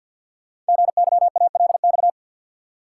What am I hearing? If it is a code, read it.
Morse code, heard as a keyed tone, about 48 wpm, 700 Hz. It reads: GYALX
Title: Code Groups - 5 Letters - 50wpm